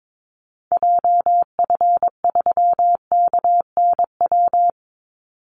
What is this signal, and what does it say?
Morse code, heard as a keyed tone, about 22 wpm, 700 Hz. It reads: JF3KNW
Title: Call Signs Repeated 3x - 22wpm